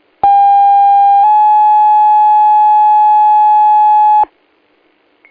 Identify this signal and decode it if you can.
Two-tone page, heard as a one-second tone, then a three-second tone, about 800 then 850 Hz. It is heard from 0.2 to 4.2 s.